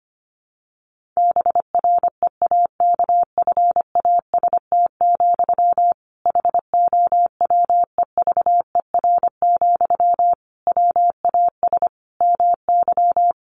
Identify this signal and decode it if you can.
Morse code, heard as a keyed tone, about 25 wpm, 700 Hz. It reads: BREAKFAST, HOWEVER, WAS MY
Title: Sentences from Top 1500 Words 25wpm